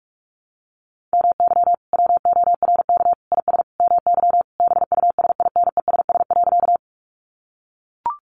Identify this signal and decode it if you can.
Morse code, heard as a keyed tone, about 44 wpm, 700 Hz. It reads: MY WORK IS MY BUSINESS.
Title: Sentences from Top 600 Words 45wpm